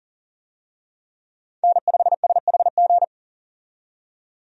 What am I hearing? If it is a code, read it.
Morse code, heard as a keyed tone, about 40 wpm, 700 Hz. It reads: N5SHG
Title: USA Call Signs - 5 Characters - 40wpm